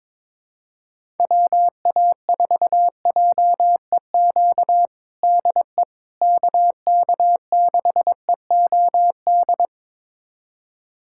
Morse code, heard as a keyed tone, about 22 wpm, 700 Hz. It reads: WA4JEQ DE KK6EOD
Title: USA Callsign DE Callsign 22wpm